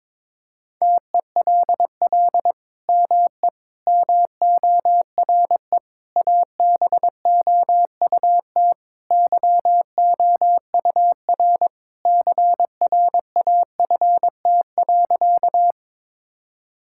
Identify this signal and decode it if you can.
Morse code, heard as a keyed tone, about 22 wpm, 700 Hz. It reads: TELL ME MORE ABOUT YOUR CRAFT.
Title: Sentences from Top 1000 Words - Review Part 6 - 22wpm